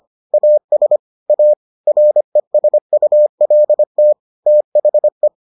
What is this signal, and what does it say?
Morse code, heard as a keyed tone, about 25 wpm, 600 Hz. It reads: AS A RESULT THE